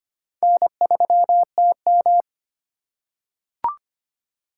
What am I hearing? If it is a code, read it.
Morse code, heard as a keyed tone, about 25 wpm, 700 Hz. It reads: N3TM
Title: USA Call Signs - 4 Characters - 3x Character Spacing - 25wpm